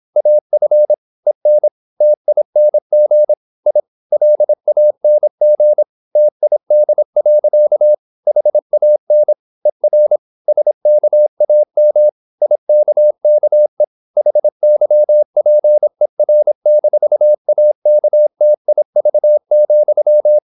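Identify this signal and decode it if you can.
Morse code, heard as a keyed tone, about 26 wpm, 600 Hz. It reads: AF EN TING I LANG TID. HAN ER SKAM IKKE HYPER-AKTIV,
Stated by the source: High-speed (QRQ) Morse at 26 wpm.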